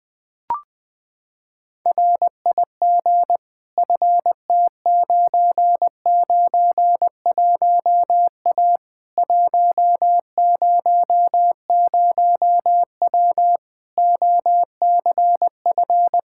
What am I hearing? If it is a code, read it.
Morse code, heard as a keyed tone, about 20 wpm, 700 Hz. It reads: RIG FT991A 100W OCF
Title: Practice QSO - Exchange 2 - Info - 20wpm